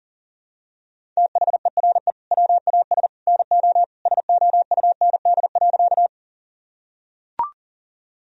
Morse code, heard as a keyed tone, about 40 wpm, 700 Hz. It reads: THERE WAS NO SOUND.
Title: Sentences from Top 1000 Words - Review Part 1 - 40wpm